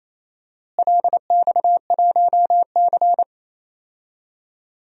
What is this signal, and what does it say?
Morse code, heard as a keyed tone, about 28 wpm, 700 Hz. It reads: LX1C
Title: DXCC-100 Call Signs - 4 Characters - Repeated 3x - 28wpm